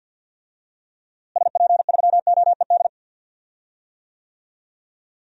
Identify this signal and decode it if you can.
Morse code, heard as a keyed tone, about 51 wpm, 700 Hz. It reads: SP3OED
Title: DXCC-100 Call Signs - 6 Characters - Repeated 3x - 50wpm